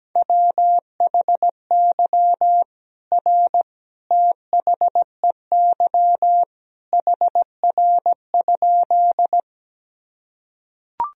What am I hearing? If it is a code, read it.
Morse code, heard as a keyed tone, about 17 wpm, 700 Hz. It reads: WHY R THEY HR?
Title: Sentences from Top 200 Words and Abbreviations 17wpm